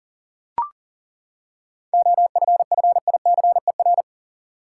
Morse code, heard as a keyed tone, about 38 wpm, 700 Hz. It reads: OFFICER
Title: Seven Letter Words - 40wpm